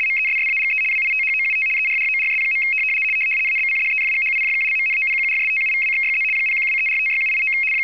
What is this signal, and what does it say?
RTTY, about 75 baud. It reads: MESSAGE WITHOUT ERRORS. ABCDEFGHIJKLMNOPQRSTUVWXYZ 0123456789
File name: w ry425 / rtty75